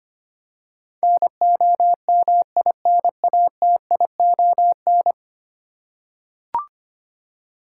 Morse code, heard as a keyed tone, about 25 wpm, 700 Hz. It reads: NOMINATION
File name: Ten Letter Words - 25wpm